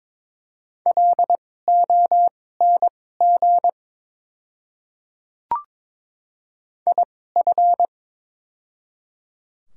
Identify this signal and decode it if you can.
Morse code, heard as a keyed tone, about 22 wpm, spaced as for 16 wpm, 700 Hz. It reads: LONG IF
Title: 2x Character Spacing - Top 100 Words - 22wpm